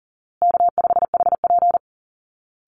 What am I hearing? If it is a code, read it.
Morse code, heard as a keyed tone, about 40 wpm, 700 Hz. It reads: K5HP
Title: Call Signs Repeated 2x - 40wpm